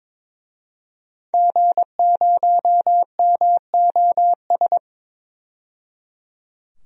Morse code, heard as a keyed tone, about 22 wpm, 700 Hz. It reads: G0MOS
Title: DXCC-100 Call Signs - 5 Characters - 22wpm